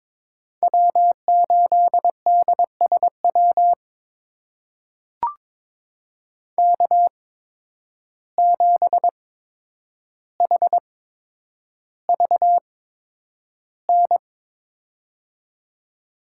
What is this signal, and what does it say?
Morse code, heard as a keyed tone, about 22 wpm, spaced as for 9 wpm, 700 Hz. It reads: W8DSW K7HVN
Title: USA Call Signs - 5 Characters - 8x Character Spacing - 22wpm